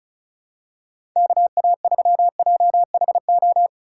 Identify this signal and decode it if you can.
Morse code, heard as a keyed tone, about 36 wpm, 700 Hz. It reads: KA3JHO
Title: USA Call Signs - 6 Characters - 2x Character Spacing - 35wpm